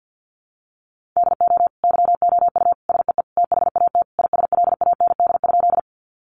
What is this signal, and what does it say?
Morse code, heard as a keyed tone, about 50 wpm, 700 Hz. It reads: DO YOU SEE THAT ISLAND?
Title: Sentences from Top 500 Words 50wpm